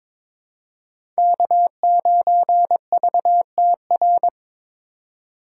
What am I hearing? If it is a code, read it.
Morse code, heard as a keyed tone, about 23 wpm, 700 Hz. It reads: K9VTR
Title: USA Call Signs - 5 Characters - 4x Character Spacing - 22wpm